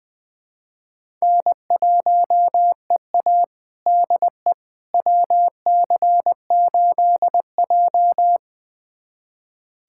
Morse code, heard as a keyed tone, about 21 wpm, 700 Hz. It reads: N1EA DE WC8J
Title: USA Callsign DE Callsign - 4 Characters - 20wpm